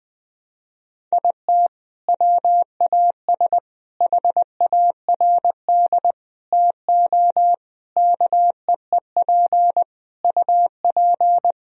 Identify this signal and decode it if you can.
Morse code, heard as a keyed tone, about 21 wpm, 700 Hz. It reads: IT WAS HARD TO KEEP UP
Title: Sentences from Top 1000 Words - Review Part 1 - 20wpm